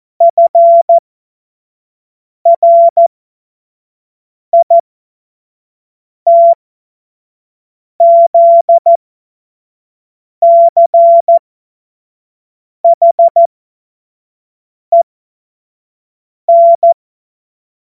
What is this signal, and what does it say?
Morse code, heard as a keyed tone, about 14 wpm, spaced as for 5 wpm, 680 Hz. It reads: FRITZCHEN
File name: Morsekurs / Witze / Witz 21. September 2021, 25 F70